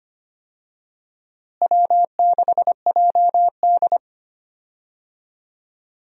Morse code, heard as a keyed tone, about 25 wpm, 700 Hz. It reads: W6JD
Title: USA Call Signs - 4 Characters - Repeated 3x - 25wpm